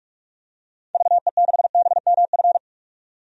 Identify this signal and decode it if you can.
Morse code, heard as a keyed tone, about 46 wpm, 700 Hz. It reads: VE6BMF
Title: DXCC-100 Call Signs - 6 Characters - 45wpm